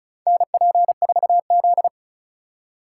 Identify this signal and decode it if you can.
Morse code, heard as a keyed tone, about 34 wpm, 700 Hz. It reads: NP4Z
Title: USA Call Signs - 2x Character Spacing - 35wpm